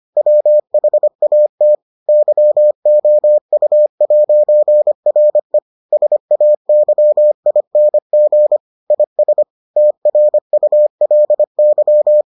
Morse code, heard as a keyed tone, about 25 wpm, 600 Hz. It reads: WHAT YOU'RE SAYING IS TRULY